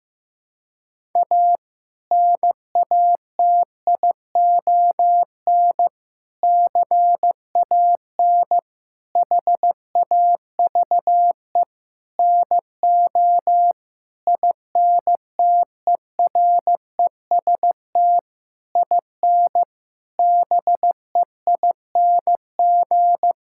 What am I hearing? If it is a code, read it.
Morse code, heard as a keyed tone, about 15 wpm, 700 Hz. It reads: A NATION CAN HAVE NO INTEREST IN BEING
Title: Sentences from Top 1000 Words - Review Part 6 - 15wpm